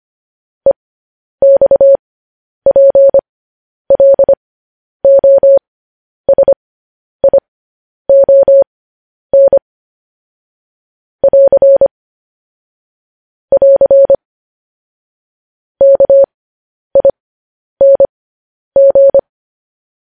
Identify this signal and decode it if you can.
Morse code, heard as a keyed tone, about 25 wpm, spaced as for 10 wpm, 550 Hz. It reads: EXPLOSION <AR> <AR> KING
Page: News Headlines in Morse Code at 10 WPM